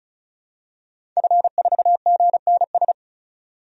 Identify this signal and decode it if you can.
Morse code, heard as a keyed tone, about 36 wpm, 700 Hz. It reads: F4GNS
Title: DXCC-100 Call Signs - 5 Characters - 35wpm